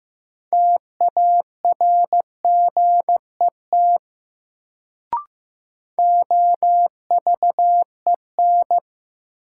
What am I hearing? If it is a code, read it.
Morse code, heard as a keyed tone, about 15 wpm, 700 Hz. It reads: TARGET OVEN
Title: Sentences from Top 1200 Words - New Word Drill Alt - 15wpm